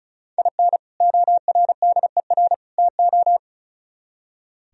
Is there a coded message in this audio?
Yes: Morse code from 0.4 to 3.4 s, heard as a keyed tone, about 44 wpm, 700 Hz. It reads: IN ORDER TO